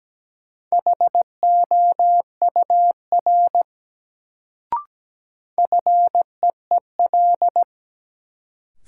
Morse code, heard as a keyed tone, about 17 wpm, 700 Hz. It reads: HOUR FEEL
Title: Sets of 5 Words - New Word Drill - 17wpm